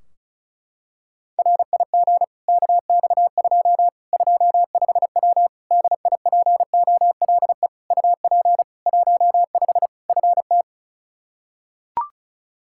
Morse code, heard as a keyed tone, about 34 wpm, 700 Hz. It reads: RIG KX2 25W DIPOLE UP 15 FT
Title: Practice QSO - Exchange 2 - Info - 35wpm